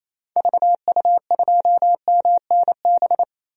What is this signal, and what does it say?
Morse code, heard as a keyed tone, about 28 wpm, 700 Hz. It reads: VU2MNB
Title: DXCC-100 Call Signs - 6 Characters - 28wpm